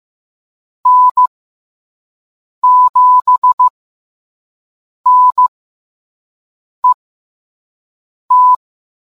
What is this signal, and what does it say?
Morse code, heard as a keyed tone, about 15 wpm, spaced as for 5 wpm, 1000 Hz. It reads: N7NET